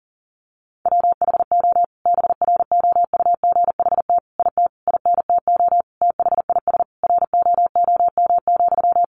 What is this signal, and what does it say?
Morse code, heard as a keyed tone, about 40 wpm, 700 Hz. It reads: WHO BROUGHT IT INTO THIS ROOM,
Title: Sentences from Top 600 Words 40wpm